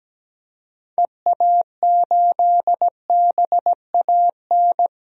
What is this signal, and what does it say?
Morse code, heard as a keyed tone, about 17 wpm, 700 Hz. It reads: EA8BAN
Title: DXCC-100 Call Signs - 6 Characters - 17wpm